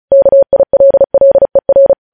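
Morse code, heard as a keyed tone, about 36 wpm, 550 Hz. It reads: KILLER
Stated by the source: Morse code news headlines sent at 35.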